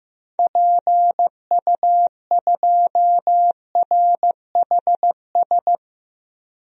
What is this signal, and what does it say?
Morse code, heard as a keyed tone, about 15 wpm, 700 Hz. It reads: PU2RHS
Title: DXCC-100 Call Signs - 6 Characters - 15wpm